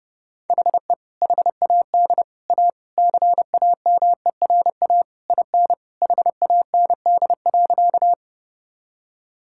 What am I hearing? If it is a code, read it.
Morse code, heard as a keyed tone, about 32 wpm, 700 Hz. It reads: HE HAD A CAMERA IN HAND.
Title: Sentences from Top 1000 Words - Review Part 5 - 30wpm